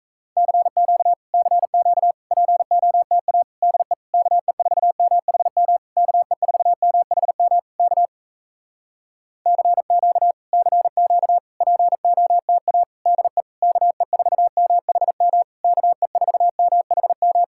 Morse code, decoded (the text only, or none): CQ CQ POTA DE KE4MHM KE4MHM K CQ CQ POTA DE KE4MHM KE4MHM